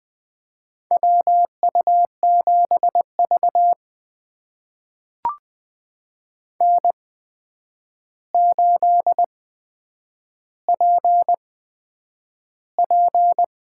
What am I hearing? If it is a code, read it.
Morse code, heard as a keyed tone, about 20 wpm, spaced as for 9 wpm, 700 Hz. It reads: WU7V N8PP
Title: USA Call Signs - 4 Characters - 8x Character Spacing - 20wpm